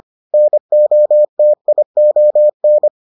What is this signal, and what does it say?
Morse code, heard as a keyed tone, about 25 wpm, 600 Hz. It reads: NOTION